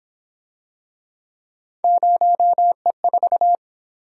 Morse code, heard as a keyed tone, about 26 wpm, 700 Hz. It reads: 0E4
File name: Speed-Racing - Three Letter-Number Combinations - 22wpm